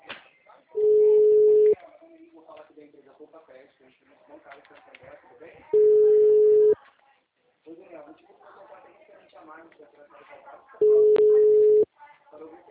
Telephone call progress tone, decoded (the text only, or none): ringback tone